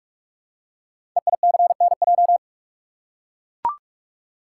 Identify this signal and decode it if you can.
Morse code, heard as a keyed tone, about 44 wpm, 700 Hz. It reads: EICNJ
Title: Code Groups - 5 Letters - 45wpm